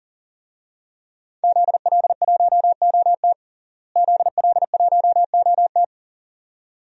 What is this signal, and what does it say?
Morse code, heard as a keyed tone, about 40 wpm, 700 Hz. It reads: ZL1OT ZL1OT
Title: DXCC-100 Call Signs - 5 Characters - Repeated 2x - 40wpm